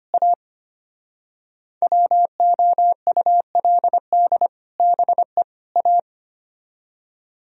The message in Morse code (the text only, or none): A WOULD BE A